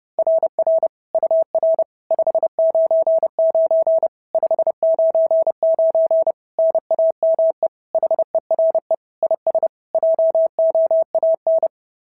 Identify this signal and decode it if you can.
Morse code, heard as a keyed tone, about 32 wpm, 690 Hz. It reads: RR UR 599 599 NAME HERE IS JOAN
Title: SOTA QSO Practice 30wpm